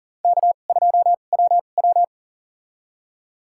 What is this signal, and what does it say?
Morse code, heard as a keyed tone, about 40 wpm, spaced as for 29 wpm, 700 Hz. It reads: K2WW
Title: USA Call Signs - 4 Characters - 2x Character Spacing - 40wpm